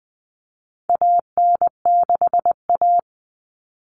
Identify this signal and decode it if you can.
Morse code, heard as a keyed tone, about 20 wpm, 700 Hz. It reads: AN6A
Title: All Cut Numbers - 4-Digit 20wpm